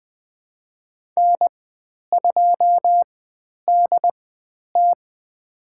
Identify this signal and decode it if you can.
Morse code, heard as a keyed tone, about 20 wpm, spaced as for 10 wpm, 700 Hz. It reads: N2DT
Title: 095 Lesson 16g - Letter c - Call Signs 20wpm